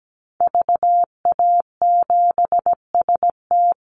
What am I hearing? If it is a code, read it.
Morse code, heard as a keyed tone, about 17 wpm, 700 Hz. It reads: VA7ST
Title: Call Signs Repeated 3x - 17wpm